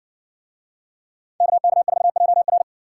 Morse code, heard as a keyed tone, about 60 wpm, 700 Hz. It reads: KK4JR